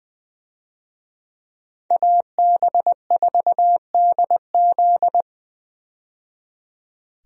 Morse code, heard as a keyed tone, about 21 wpm, 700 Hz. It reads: AB4DZ
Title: USA Call Signs - 5 Characters - 20wpm